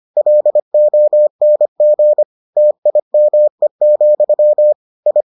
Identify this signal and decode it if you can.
Morse code, heard as a keyed tone, about 25 wpm, 600 Hz. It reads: LONG TIME, I